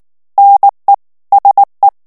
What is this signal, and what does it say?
Morse code, heard as a keyed tone, about 19 wpm, 800 Hz. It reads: NE SE
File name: CW / FICHIERS-MP3 / FRANCAIS TEXTE CLAIR 2 VITESSE 18 MOTS MINUTES